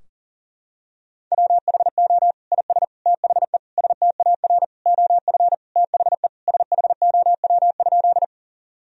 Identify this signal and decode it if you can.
Morse code, heard as a keyed tone, about 40 wpm, 700 Hz. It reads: WHO IS THE STAR OF THE SHOW?
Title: Sentences from Top 1000 Words - Review Part 2 - 40wpm